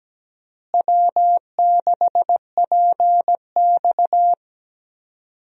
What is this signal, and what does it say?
Morse code, heard as a keyed tone, about 17 wpm, 700 Hz. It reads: W6PX
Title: USA Call Signs - 4 Characters - Repeated 3x - 17wpm